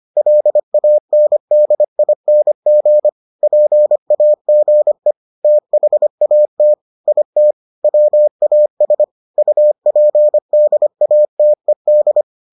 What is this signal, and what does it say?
Morse code, heard as a keyed tone, about 25 wpm, 600 Hz. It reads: LANDING PAGE THAT IT WAS UPDATED